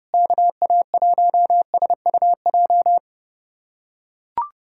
Morse code, heard as a keyed tone, about 32 wpm, 700 Hz. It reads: KA1SUJ
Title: USA Callsign DE Callsign 30wpm